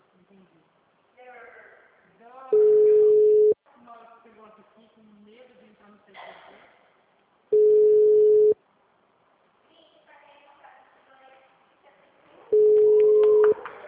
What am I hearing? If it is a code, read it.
Ringback tone (Europe, 425 Hz), with speech.
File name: out- -2016- - - .3156